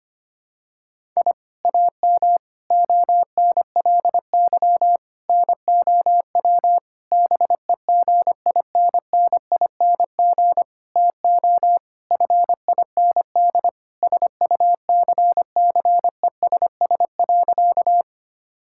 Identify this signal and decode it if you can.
Morse code, heard as a keyed tone, about 25 wpm, 700 Hz. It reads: I AM ONLY NOW BEGINNING TO FIND SUCCESS.
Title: Sentences from Top 1000 Words - Review Part 5 - 25wpm